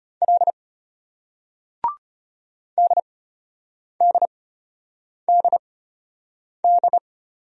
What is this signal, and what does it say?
Morse code, heard as a keyed tone, about 35 wpm, spaced as for 8 wpm, 700 Hz. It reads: L DDDD